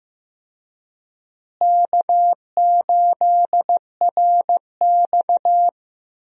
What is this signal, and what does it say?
Morse code, heard as a keyed tone, about 15 wpm, 700 Hz. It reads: K8RX
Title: USA Call Signs - 4 Characters - Repeated 2x - 15wpm